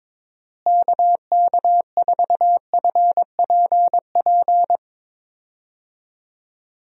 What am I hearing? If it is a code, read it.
Morse code, heard as a keyed tone, about 22 wpm, 700 Hz. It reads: KK4FPP
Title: USA Call Signs - 6 Characters - Repeated 3x - 22wpm